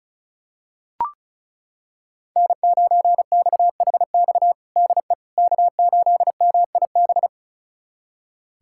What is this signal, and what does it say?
Morse code, heard as a keyed tone, about 36 wpm, 690 Hz. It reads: N9XHX DE K8MIB
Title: USA Callsign DE Callsign - 5 Characters - 35wpm